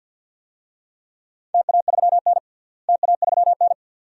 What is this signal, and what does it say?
Morse code, heard as a keyed tone, about 50 wpm, 700 Hz. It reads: TA3N TA3N
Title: DXCC-100 Call Signs - 4 Characters - Repeated 2x - 50wpm